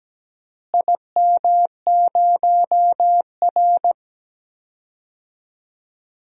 Morse code, heard as a keyed tone, about 17 wpm, 700 Hz. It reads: IM0R
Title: DXCC-100 Call Signs - 4 Characters - Repeated 3x - 17wpm